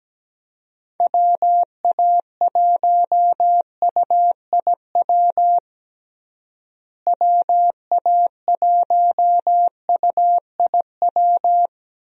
Morse code, heard as a keyed tone, about 17 wpm, 700 Hz. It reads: WA1UIW WA1UIW
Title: USA Call Signs - 6 Characters - Repeated 3x - 17wpm